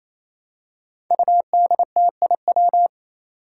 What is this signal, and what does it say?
Morse code, heard as a keyed tone, about 28 wpm, 700 Hz. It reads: UDTIW
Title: Five Challenging Letter-Number Combinations 28wpm